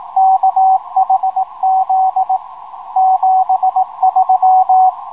Morse code, heard as a keyed tone, about 18 wpm, 780 Hz, with radio noise. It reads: KHZ 73
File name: rs-12 robot